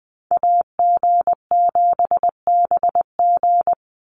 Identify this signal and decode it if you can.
Morse code, heard as a keyed tone, about 20 wpm, 700 Hz. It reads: AG7BG
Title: USA Call Signs 20wpm